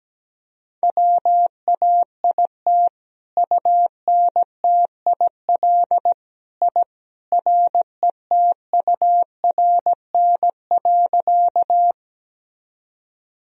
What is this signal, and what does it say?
Morse code, heard as a keyed tone, about 17 wpm, 700 Hz. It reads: WAIT UNTIL I RETURN.
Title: Sentences from Top 1000 Words - Review Part 1 - 17wpm